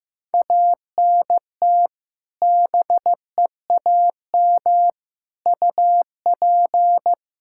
Morse code, heard as a keyed tone, about 15 wpm, 700 Hz. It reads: ANT BEAM UP
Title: Practice QSO - Exchange 2 - Info - 15wpm